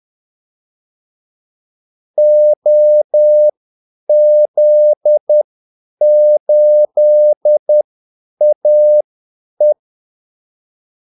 Morse code, heard as a keyed tone, about 10 wpm, spaced as for 8 wpm, 600 Hz. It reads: OZ8AE
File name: Callsigns-10-8wpm